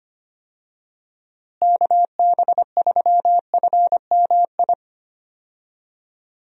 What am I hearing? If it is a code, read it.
Morse code, heard as a keyed tone, about 25 wpm, 700 Hz. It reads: KB3FMI
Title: USA Call Signs - 6 Characters - Repeated 2x - 25wpm